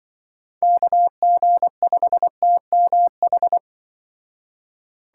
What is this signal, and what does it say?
Morse code, heard as a keyed tone, about 24 wpm, 700 Hz. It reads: KG5TMH
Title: Speed Racing - USA 2x3 Call Signs 20wpm